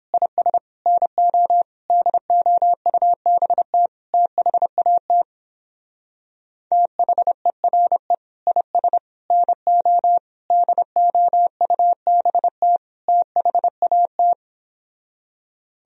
Morse code, heard as a keyed tone, about 29 wpm, 700 Hz. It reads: IS NO DOUBT THAT THERE IS NO DOUBT THAT